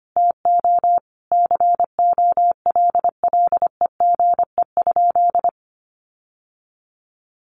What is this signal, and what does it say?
Morse code, heard as a keyed tone, about 25 wpm, 700 Hz. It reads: TO COLLEGE?
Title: Sentences from Top 1300 Words 25wpm